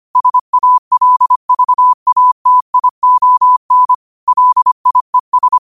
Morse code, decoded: SALVATION LIES

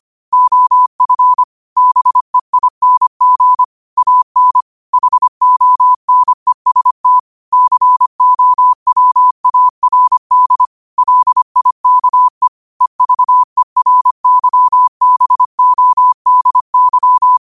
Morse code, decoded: OF BEING AN HONEST COWARD LIKE EVERYBODY

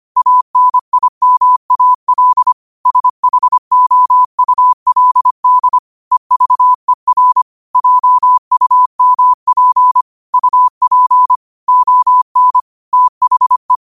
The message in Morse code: ANIMAL SHOULD EVER JUMP UP ON THE

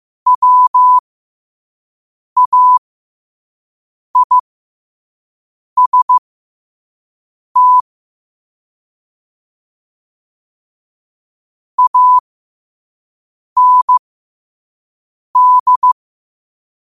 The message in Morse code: WAIST AND

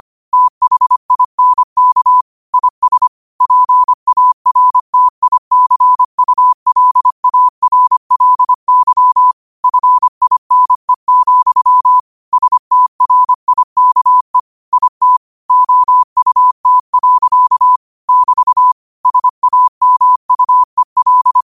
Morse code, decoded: THINK IS PARTICULARLY FINE, STRIKE IT OUT. <BT> SAMUEL

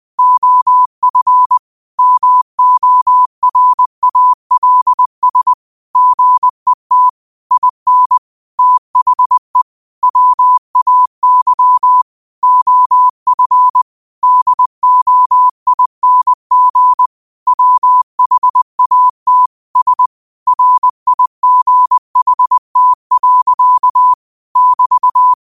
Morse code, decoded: OF MORALS GET IN THE WAY OF DOING WHAT S RIGHT. <BT>

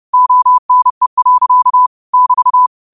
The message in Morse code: ONE. <BT>

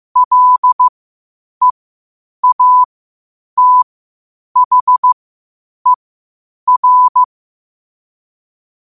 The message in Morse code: LEATHER

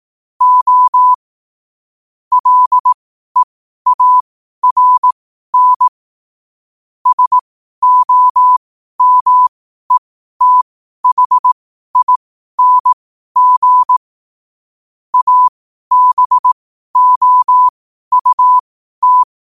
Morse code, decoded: O LEARN SOMETHING ABOUT

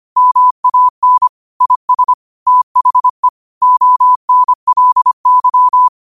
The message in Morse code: MAN IS THE ONLY

